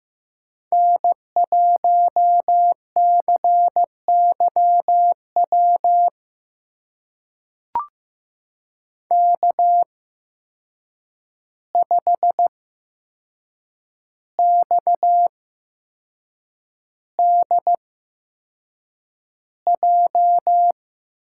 Morse code, heard as a keyed tone, about 15 wpm, spaced as for 6 wpm, 700 Hz. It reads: N1CYW K5XDJ
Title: USA Call Signs - 5 Characters - 8x Character Spacing - 15wpm